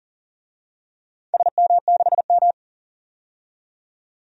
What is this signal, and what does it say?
Morse code, heard as a keyed tone, about 40 wpm, 700 Hz. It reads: SM6M
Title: DXCC-100 Call Signs - 4 Characters - 40wpm